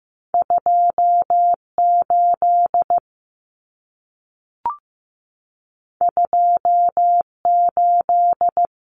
Morse code, heard as a keyed tone, about 15 wpm, 700 Hz. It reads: 28 28
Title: Numbers - 2-Digit 15wpm